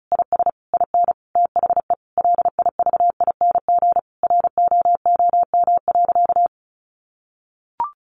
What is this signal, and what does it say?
Morse code, heard as a keyed tone, about 34 wpm, 700 Hz. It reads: IS IN THE LIVING ROOM.
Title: Sentences from Top 1200 Words 35wpm